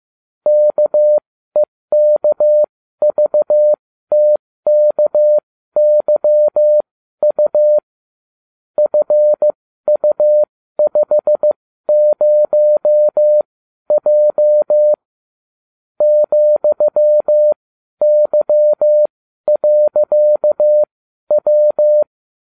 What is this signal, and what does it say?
Morse code, heard as a keyed tone, about 15 wpm, spaced as for 12 wpm, 600 Hz. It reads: KEKVTKYU FU50J ,Y.W